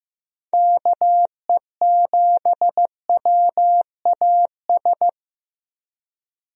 Morse code, heard as a keyed tone, about 15 wpm, 700 Hz. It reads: KE7WAS
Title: USA Call Signs - 6 Characters - Repeated 3x - 15wpm